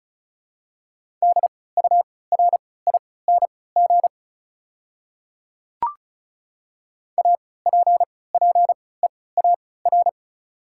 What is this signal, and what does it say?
Morse code, heard as a keyed tone, about 36 wpm, spaced as for 20 wpm, 690 Hz. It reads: DURING APPEAR
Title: 3x Character Spacing - Top 500 Words - 35wpm